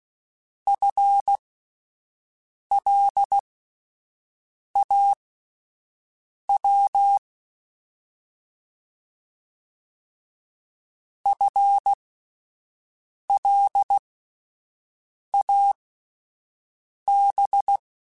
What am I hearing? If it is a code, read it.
Morse code, heard as a keyed tone, about 16 wpm, spaced as for 5 wpm, 800 Hz. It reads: FLAW FLAB
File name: Alphabetical / Alphabetical mp3s / Fs @5